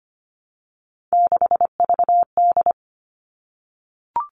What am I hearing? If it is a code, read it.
Morse code, heard as a keyed tone, about 25 wpm, 700 Hz. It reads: 6VD
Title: All Cut Numbers - 3-Digit 25wpm